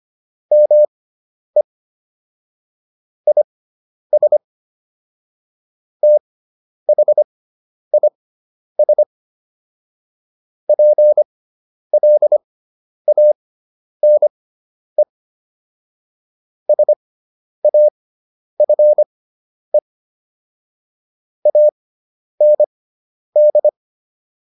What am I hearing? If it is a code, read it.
Morse code, heard as a keyed tone, about 25 wpm, spaced as for 10 wpm, 600 Hz. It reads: ME IS THIS PLANE SAFE AND